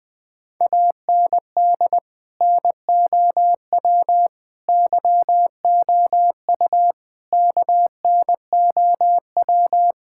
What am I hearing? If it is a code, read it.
Morse code, heard as a keyed tone, about 20 wpm, 700 Hz. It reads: AND NOW YOU KNOW
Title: Sentences from Top 1000 Words - Review Part 6 - 20wpm